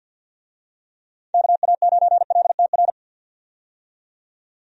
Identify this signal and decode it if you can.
Morse code, heard as a keyed tone, about 51 wpm, 700 Hz. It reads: KA9LTR
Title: USA Call Signs - 6 Characters - 50wpm